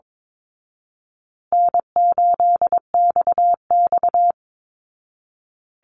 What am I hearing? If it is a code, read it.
Morse code, heard as a keyed tone, about 22 wpm, 690 Hz. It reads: N8XX
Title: Call Signs Repeated 2x - 22wpm